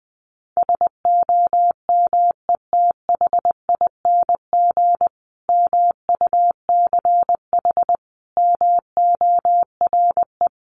Morse code, heard as a keyed tone, about 20 wpm, 700 Hz. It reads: SOMETHING MUCH MORE
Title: A Sound Of Thunder 20wpm